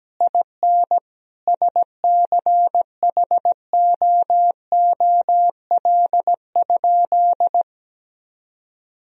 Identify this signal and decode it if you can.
Morse code, heard as a keyed tone, about 17 wpm, 700 Hz. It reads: IN SCHOOL?
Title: Sentences from Top 1000 Words - Review Part 3 - 17wpm